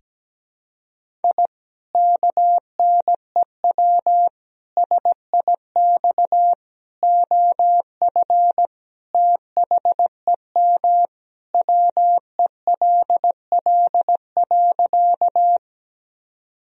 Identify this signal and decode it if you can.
Morse code, heard as a keyed tone, about 17 wpm, 700 Hz. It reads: I KNEW SIX OF THEM WELL.
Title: Sentences from Top 1000 Words - Review Part 2 - 17wpm